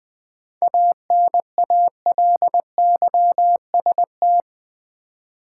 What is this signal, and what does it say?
Morse code, heard as a keyed tone, about 20 wpm, 700 Hz. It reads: ANALYST
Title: K1USN SST QSO Practice 20wpm